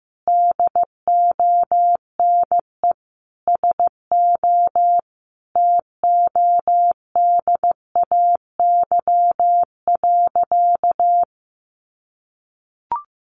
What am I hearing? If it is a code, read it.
Morse code, heard as a keyed tone, about 15 wpm, 700 Hz. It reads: DONE SO TODAY.
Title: Sentences from Top 1500 Words 15wpm